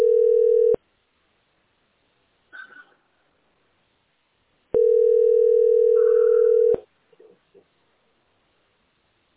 Ringback tone (North America), 440 + 480 Hz.